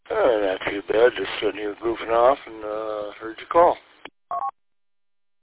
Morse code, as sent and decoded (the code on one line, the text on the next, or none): none